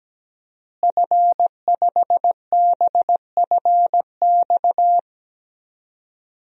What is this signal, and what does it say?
Morse code, heard as a keyed tone, about 17 wpm, 700 Hz. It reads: F5BFX
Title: DXCC-100 Call Signs - 5 Characters - Repeated 3x - 17wpm